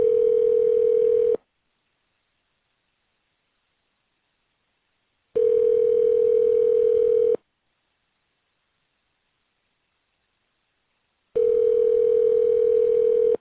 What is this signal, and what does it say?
Ringback tone (North America), 440 + 480 Hz.